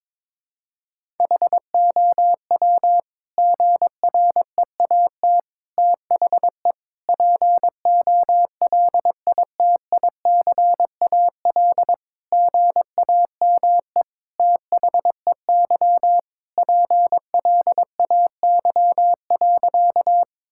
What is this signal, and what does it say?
Morse code, heard as a keyed tone, about 22 wpm, 700 Hz. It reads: HOW GREAT THE POLITICAL GAME THEY PLAY.
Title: Sentences from Top 1000 Words - Review Part 5 - 22wpm